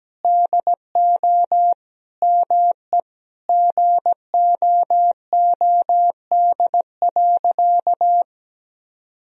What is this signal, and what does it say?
Morse code, heard as a keyed tone, about 17 wpm, 700 Hz. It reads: DO ME GOOD.
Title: Sentences from Top 1000 Words - Review Part 1 - 17wpm